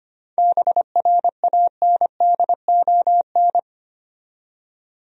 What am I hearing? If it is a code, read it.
Morse code, heard as a keyed tone, about 25 wpm, 700 Hz. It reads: BRANDON
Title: USA Common Names - 25wpm